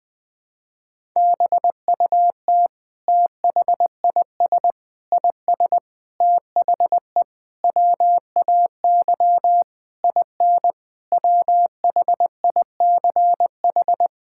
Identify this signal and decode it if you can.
Morse code, heard as a keyed tone, about 20 wpm, 700 Hz. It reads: BUT THIS IS THE WAY IN WHICH
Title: Sentences from Top 1000 Words - Review Part 5 - 20wpm